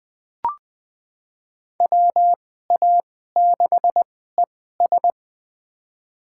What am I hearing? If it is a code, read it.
Morse code, heard as a keyed tone, about 21 wpm, spaced as for 15 wpm, 700 Hz. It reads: WA6ES
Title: USA Call Signs - 5 Characters - 2x Character Spacing - 20wpm